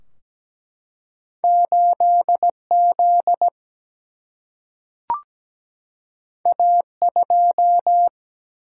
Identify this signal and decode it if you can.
Morse code, heard as a keyed tone, about 17 wpm, 700 Hz. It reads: 8Z A2
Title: Single-Letter Single-Number 17wpm